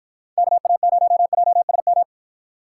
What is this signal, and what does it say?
Morse code, heard as a keyed tone, about 53 wpm, 700 Hz. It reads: KA0JSM